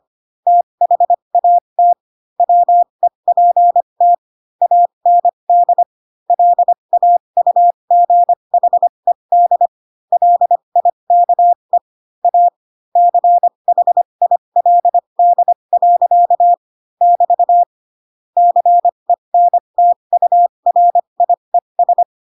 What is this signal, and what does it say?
Morse code, heard as a keyed tone, about 25 wpm, spaced as for 22 wpm, 700 Hz. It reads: THAT WEPT AND LAUGHED LIKE A CHILD. <BT> CENTURIES